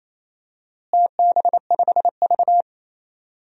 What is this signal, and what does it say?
Morse code, heard as a keyed tone, about 28 wpm, 700 Hz. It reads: TB5V
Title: DXCC-100 Call Signs - 4 Characters - Repeated 2x - 28wpm